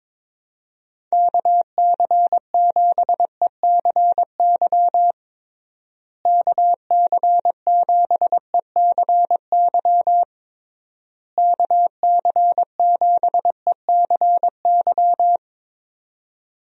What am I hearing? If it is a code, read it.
Morse code, heard as a keyed tone, about 22 wpm, 700 Hz. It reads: KC7ECY KC7ECY KC7ECY
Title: USA Call Signs - 6 Characters - Repeated 3x - 22wpm